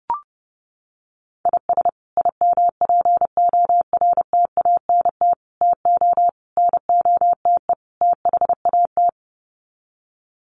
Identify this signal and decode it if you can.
Morse code, heard as a keyed tone, about 30 wpm, 700 Hz. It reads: IS IMPORTANT TO NOTE THAT
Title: Sets of 5 Words 30wpm